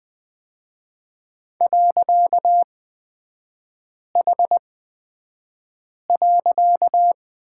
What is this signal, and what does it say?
Morse code, heard as a keyed tone, about 20 wpm, spaced as for 5 wpm, 700 Hz. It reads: .H.
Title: 255 Lesson 41e - Period - Three Characters 20wpm